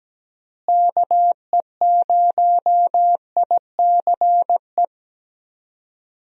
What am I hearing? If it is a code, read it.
Morse code, heard as a keyed tone, about 17 wpm, 700 Hz. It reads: KE0ICE
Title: USA Call Signs - 6 Characters - Repeated 3x - 17wpm